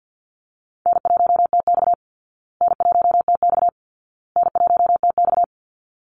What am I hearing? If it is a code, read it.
Morse code, heard as a keyed tone, about 48 wpm, 700 Hz. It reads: N1TX N1TX N1TX
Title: Easy Call Signs Repeated 3x - 50wpm